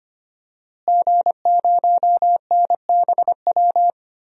Morse code, heard as a keyed tone, about 25 wpm, 700 Hz. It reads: G0NBW
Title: DXCC-100 Call Signs - 5 Characters - Repeated 3x - 25wpm